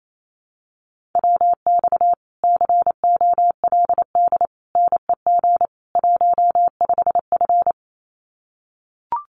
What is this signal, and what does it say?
Morse code, heard as a keyed tone, about 28 wpm, 700 Hz. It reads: WX COLD NEG 15F
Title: Common QSO Phrases Encore 28wpm